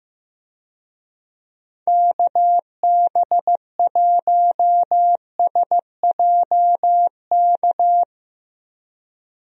Morse code, heard as a keyed tone, about 15 wpm, 700 Hz. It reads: KB1SJK
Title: USA Call Signs - 6 Characters - 15wpm